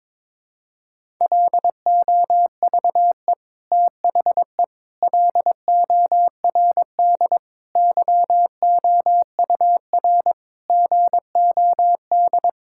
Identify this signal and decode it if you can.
Morse code, heard as a keyed tone, about 22 wpm, 700 Hz. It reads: LOVE THE LORD YOUR GOD
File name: Sentences with 9-Word Phrases - 22wpm